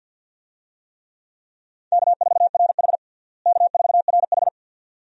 Morse code, heard as a keyed tone, about 48 wpm, 700 Hz. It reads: K4RH K4RH
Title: USA Call Signs - 4 Characters - Repeated 2x - 50wpm